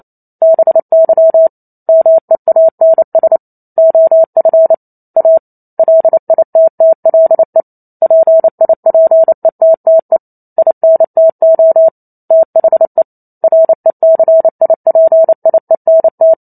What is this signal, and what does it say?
Morse code, heard as a keyed tone, about 28 wpm, 650 Hz. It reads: BY MEANS OF A LITTLE PIPETTE INTO THE RECIPIENT
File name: Chapter 2 - 28 WPM